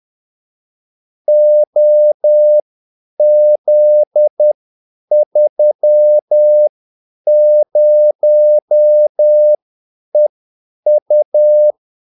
Morse code, heard as a keyed tone, about 10 wpm, spaced as for 8 wpm, 600 Hz. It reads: OZ30EU
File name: Callsigns-10-8wpm